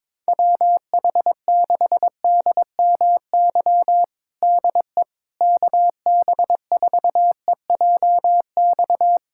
Morse code, heard as a keyed tone, about 22 wpm, 700 Hz. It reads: WH6DMY DE KB4EJX
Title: USA Callsign DE Callsign - 6 Characters - 22wpm